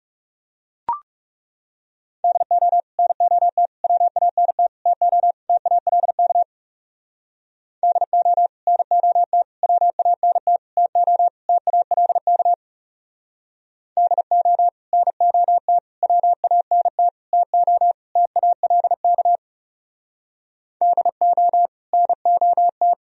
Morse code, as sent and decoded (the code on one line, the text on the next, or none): -.. --- / -. --- - / .-- .- -. - / - --- / - .- .-.. -.- / -.. --- / -. --- - / .-- .- -. - / - --- / - .- .-.. -.- / -.. --- / -. --- - / .-- .- -. - / - --- / - .- .-.. -.- / -.. --- / -. --- -
DO NOT WANT TO TALK DO NOT WANT TO TALK DO NOT WANT TO TALK DO NOT